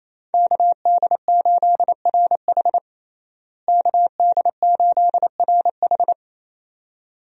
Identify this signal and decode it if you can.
Morse code, heard as a keyed tone, about 28 wpm, 700 Hz. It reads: KD8RH KD8RH
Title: USA Call Signs - 5 Characters - Repeated 2x - 28wpm